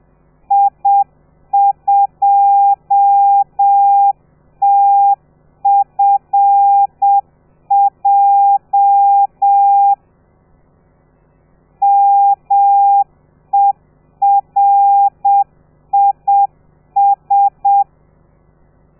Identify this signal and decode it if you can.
Morse code, heard as a keyed tone, about 7 wpm, 800 Hz. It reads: I2TFJ MERIS
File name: telegrafia / i2tfj / Lez13 30